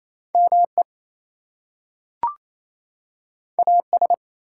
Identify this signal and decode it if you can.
Morse code, heard as a keyed tone, about 28 wpm, 700 Hz. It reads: ME AS
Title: Two Letter Words - 28wpm